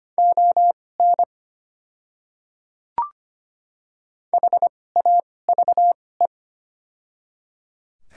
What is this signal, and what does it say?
Morse code, heard as a keyed tone, about 25 wpm, spaced as for 18 wpm, 690 Hz. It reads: ON HAVE
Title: 2x Character Spacing - Top 100 Words - 25wpm